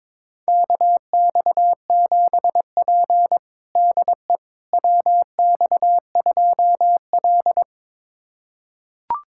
Morse code, heard as a keyed tone, about 22 wpm, 700 Hz. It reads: KX7P DE WX2L
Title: USA Callsign DE Callsign - 4 Characters - 22wpm